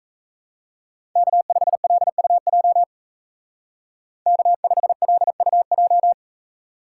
Morse code, heard as a keyed tone, about 34 wpm, 700 Hz. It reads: K5LUJ K5LUJ
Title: Speed Racing - USA Call Signs 28wpm